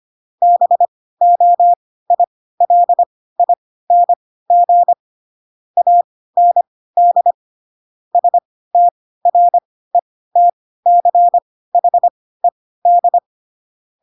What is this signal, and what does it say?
Morse code, heard as a keyed tone, about 25 wpm, spaced as for 16 wpm, 700 Hz. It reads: BOILING AND STRETCHED